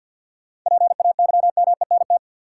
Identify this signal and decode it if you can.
Morse code, heard as a keyed tone, about 53 wpm, 700 Hz. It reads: PAYMENT
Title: Sentences from Top 1200 Words - New Word Drill Alt - 50wpm